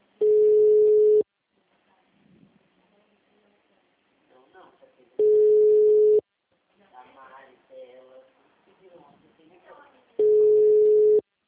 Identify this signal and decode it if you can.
Ringback tone (Europe, 425 Hz), with speech.